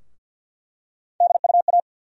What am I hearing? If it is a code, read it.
Morse code, heard as a keyed tone, about 50 wpm, 700 Hz. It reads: DUA
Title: USA Call Sign Postfixes - 3 Letters 50wpm